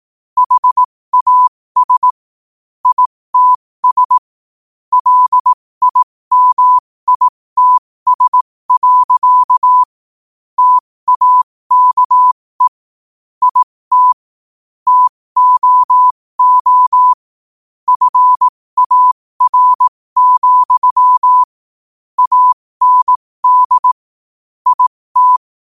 Morse code, heard as a keyed tone, about 18 wpm, spaced as for 15 wpm, 1000 Hz. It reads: HAS ITS LIMITS. TAKE IT TOO FAR, AND IT
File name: qotdfiles / 15wpm